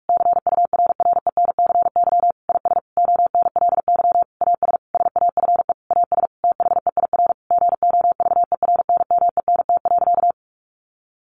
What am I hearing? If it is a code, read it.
Morse code, heard as a keyed tone, about 44 wpm, 700 Hz. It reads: CURRENCY IS ONLY AS SAFE AS THEIR GOVERNMENT.
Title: Sentences from Top 1500 Words 45wpm